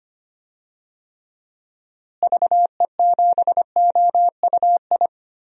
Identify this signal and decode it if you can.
Morse code, heard as a keyed tone, about 25 wpm, 700 Hz. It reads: VE7OUI
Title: DXCC-100 Call Signs - 6 Characters - Repeated 3x - 25wpm